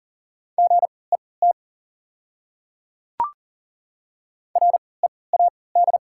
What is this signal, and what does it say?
Morse code, heard as a keyed tone, about 40 wpm, spaced as for 23 wpm, 690 Hz. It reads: GET READ
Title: 3x Character Spacing - Top 200 Words - 40wpm